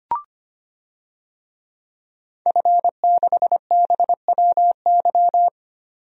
Morse code, heard as a keyed tone, about 25 wpm, 700 Hz. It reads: F6BWY
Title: DXCC-100 Call Signs - 5 Characters - 25wpm